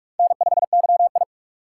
Morse code, heard as a keyed tone, about 47 wpm, 700 Hz. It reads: N5YI
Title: USA Call Signs - 4 Characters - 3x Character Spacing - 45wpm